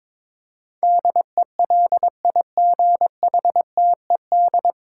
Morse code, heard as a keyed tone, about 22 wpm, 700 Hz. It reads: DELIGHTED
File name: 9 Letter Words - First 3 Letters Given - 22wpm